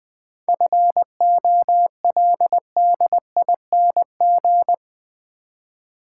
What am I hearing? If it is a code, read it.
Morse code, heard as a keyed tone, about 20 wpm, 700 Hz. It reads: FOLDING
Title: 7 Letter Words - First 3 Letters Given - 20wpm